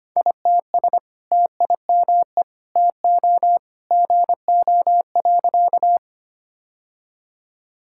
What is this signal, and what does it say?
Morse code, heard as a keyed tone, about 25 wpm, 700 Hz. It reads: ITS TIME TO GO.
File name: practice / 25wpm / Sentences with 2-Word Phrases - 25wpm